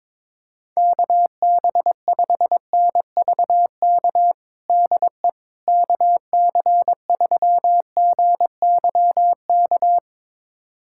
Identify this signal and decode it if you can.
Morse code, heard as a keyed tone, about 22 wpm, 700 Hz. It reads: KB5NVK DE KC3GYK
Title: USA Callsign DE Callsign 22wpm